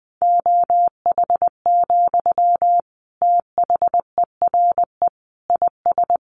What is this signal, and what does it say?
Morse code, heard as a keyed tone, about 20 wpm, 700 Hz. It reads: OH, THERE IS
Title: Sentences from Top 1500 Words 20wpm